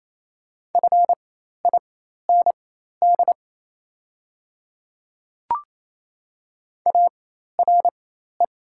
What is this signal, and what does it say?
Morse code, heard as a keyed tone, about 28 wpm, spaced as for 13 wpm, 700 Hz. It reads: FIND ARE
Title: 4x Character Spacing - Top 200 Words - 28wpm